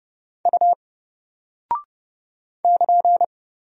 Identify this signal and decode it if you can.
Morse code, heard as a keyed tone, about 30 wpm, 700 Hz. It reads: U <KN>
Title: Single Character (Letter:Number:Punctuation:Prosign) Alt 30wpm